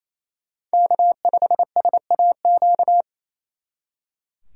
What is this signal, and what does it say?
Morse code, heard as a keyed tone, about 35 wpm, 700 Hz. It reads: K5SAQ